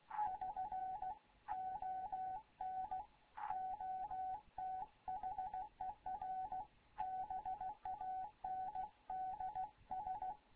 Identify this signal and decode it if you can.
Morse code, heard as a keyed tone, about 16 wpm, 750 Hz. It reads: <SN> ON OTHER BANDS
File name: Morze-HA2MN / hangp / 2 11 GYAKORLAT